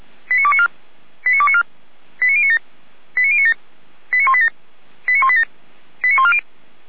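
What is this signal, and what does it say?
Five-tone selective call (ZVEI), heard as quick runs of five tones, tones of about 70 ms. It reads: 89294 89294 89097 89097 89187 89187 89140
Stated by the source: ZVEI five-tone selective call 89229 received on an SDR.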